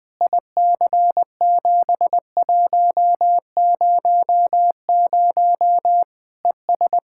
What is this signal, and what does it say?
Morse code, heard as a keyed tone, about 20 wpm, 700 Hz. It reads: IC7100 ES
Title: Practice QSO - Exchange 2 - Info - 20wpm